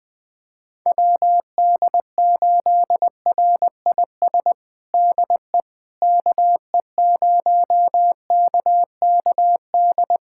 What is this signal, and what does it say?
Morse code, heard as a keyed tone, about 21 wpm, 700 Hz. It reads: WD8RIS DE KE0KKD
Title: USA Callsign DE Callsign - 6 Characters - 20wpm